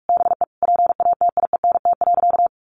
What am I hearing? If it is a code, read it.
Morse code, heard as a keyed tone, about 44 wpm, 700 Hz. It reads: BE PATIENT.
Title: Easy Phrases 45wpm